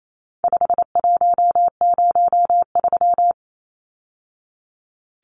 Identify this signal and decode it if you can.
Morse code, heard as a keyed tone, about 28 wpm, 700 Hz. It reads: 5103
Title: Numbers - 4-Digit 28wpm